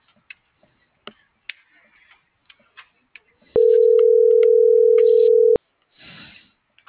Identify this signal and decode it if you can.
Ringback tone (North America), 440 + 480 Hz.
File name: CENTRAL MAERSK / out kevin.riosmsk